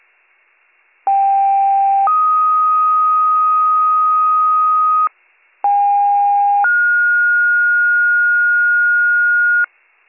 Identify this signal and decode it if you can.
Two-tone page, heard as a one-second tone, then a three-second tone, about 790 then 1250 Hz, about 810 then 1430 Hz, with noise. It is heard from 1.1 to 9.7 s.